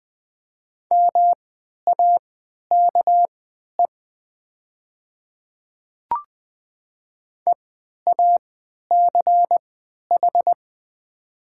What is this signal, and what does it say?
Morse code, heard as a keyed tone, about 20 wpm, spaced as for 11 wpm, 700 Hz. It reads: MAKE EACH
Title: 3x Character Spacing - Top 100 Words - 20wpm